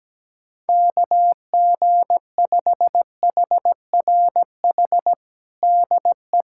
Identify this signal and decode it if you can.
Morse code, heard as a keyed tone, about 17 wpm, 700 Hz. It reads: KG5HRH DE
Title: USA Callsign DE Callsign - 6 Characters - 17wpm